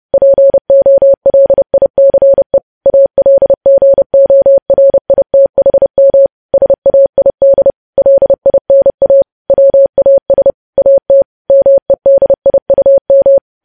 Morse code, heard as a keyed tone, about 30 wpm, 550 Hz. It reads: POLICE ALGORITHM SAID LINA WAS AT MEDIUM
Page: news headlines in morse code at 30 wpm